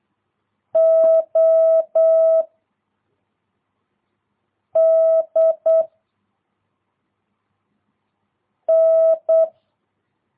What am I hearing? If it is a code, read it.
Morse code, heard as a keyed tone, about 8 wpm, 660 Hz. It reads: O D N
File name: jh8cbh / 2022cw